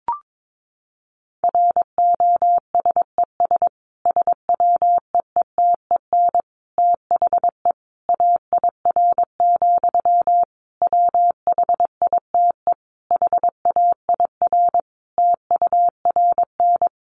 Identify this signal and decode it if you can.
Morse code, heard as a keyed tone, about 22 wpm, 700 Hz. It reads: ROSES SWEETEN THE AIR, WHITE HAIR TURN
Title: A Sound Of Thunder 22wpm